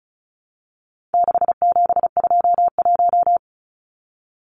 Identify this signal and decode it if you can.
Morse code, heard as a keyed tone, about 34 wpm, 700 Hz. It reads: 6721
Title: Numbers - 4-Digit 35wpm